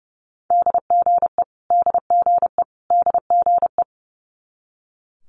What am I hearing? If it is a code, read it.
Morse code, heard as a keyed tone, about 32 wpm, 700 Hz. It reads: DGE DGE DGE
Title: Multi-letter Phonemes Repeated 3x 30wpm